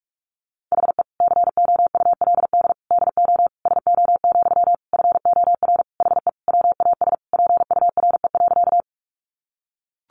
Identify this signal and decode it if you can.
Morse code, heard as a keyed tone, about 44 wpm, 700 Hz. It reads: HE COULD DO SO, FOR HE WAS PURE.
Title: Sentences from Top 1500 Words 45wpm